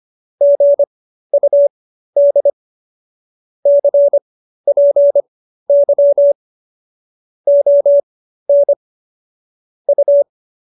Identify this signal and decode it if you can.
Morse code, heard as a keyed tone, about 25 wpm, spaced as for 13 wpm, 570 Hz. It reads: GUD CPY ON U
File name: qso215b 13